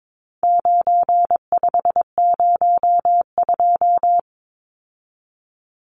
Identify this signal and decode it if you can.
Morse code, heard as a keyed tone, about 22 wpm, 700 Hz. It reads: 9502
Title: Numbers - 4-Digit 22wpm